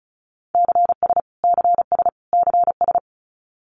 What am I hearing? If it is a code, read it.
Morse code, heard as a keyed tone, about 34 wpm, 700 Hz. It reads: CS CS CS
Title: Abbreviations Repeated 3x - 35wpm